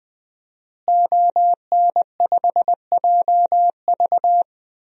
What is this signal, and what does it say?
Morse code, heard as a keyed tone, about 20 wpm, 700 Hz. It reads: ON5JV
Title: DXCC-100 Call Signs - 5 Characters - Repeated 2x - 20wpm